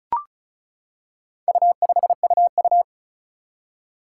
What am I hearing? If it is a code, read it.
Morse code, heard as a keyed tone, about 36 wpm, 700 Hz. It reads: U5UU
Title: Four Challenging Letter-Number Combinations 35wpm